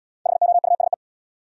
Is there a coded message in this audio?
Yes: Morse code from 0.3 to 0.9 s, heard as a keyed tone, about 73 wpm, 700 Hz. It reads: HOUSE